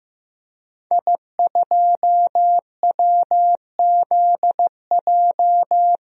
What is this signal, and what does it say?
Morse code, heard as a keyed tone, about 15 wpm, 700 Hz. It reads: I2WZJ
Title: DXCC-100 Call Signs - 5 Characters - 15wpm